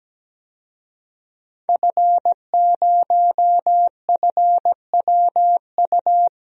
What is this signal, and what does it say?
Morse code, heard as a keyed tone, about 17 wpm, 700 Hz. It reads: F0FWU
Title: DXCC-100 Call Signs - 5 Characters - Repeated 2x - 17wpm